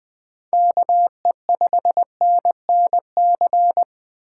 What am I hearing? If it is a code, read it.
Morse code, heard as a keyed tone, about 20 wpm, 700 Hz. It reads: KE5NNC
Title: Speed Racing - USA 2x3 Call Signs 17wpm